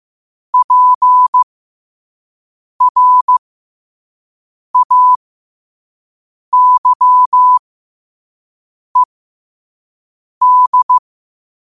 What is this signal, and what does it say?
Morse code, heard as a keyed tone, about 15 wpm, spaced as for 5 wpm, 1000 Hz. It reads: PRAYED